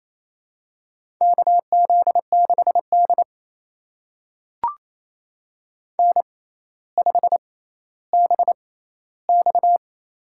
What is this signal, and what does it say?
Morse code, heard as a keyed tone, about 28 wpm, spaced as for 14 wpm, 700 Hz. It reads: KZ6D N5BX
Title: USA Call Signs - 4 Characters - 6x Character Spacing - 28wpm